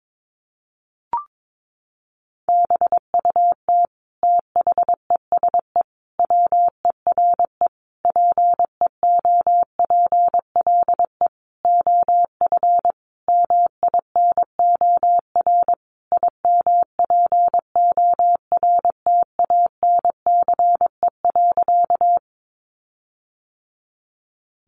Morse code, heard as a keyed tone, about 22 wpm, 700 Hz. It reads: BUT THESE WERE PEOPLE OF MINOR IMPORTANCE.
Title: Sentences from Top 1500 Words 22wpm